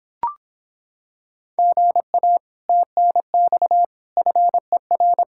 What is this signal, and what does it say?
Morse code, heard as a keyed tone, about 26 wpm, 700 Hz. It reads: GA TNX FER